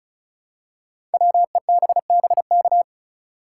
Morse code, heard as a keyed tone, about 36 wpm, 690 Hz. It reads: WEBBK
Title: Code Groups - 5 Letter:Numbers - 35wpm